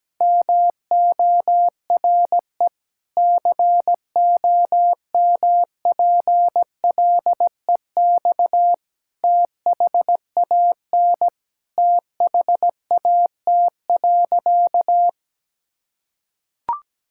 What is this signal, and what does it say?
Morse code, heard as a keyed tone, about 17 wpm, 700 Hz. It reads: MORE COMPLEX THAN THAT.
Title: Sentences from Top 1000 Words - Review Part 6 - 17wpm